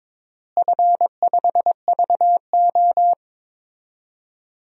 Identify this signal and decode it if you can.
Morse code, heard as a keyed tone, about 22 wpm, 700 Hz. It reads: F5VO
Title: DXCC-100 Call Signs - 4 Characters - Repeated 3x - 22wpm